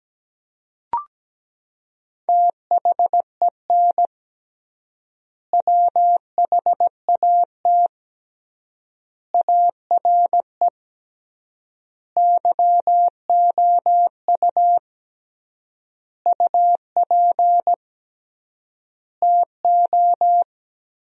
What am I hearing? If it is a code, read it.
Morse code, heard as a keyed tone, about 17 wpm, 700 Hz. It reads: THEN WHAT ARE YOU UP TO
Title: 3x Word-Spacing - Sentences from Top 100 Words - 17wpm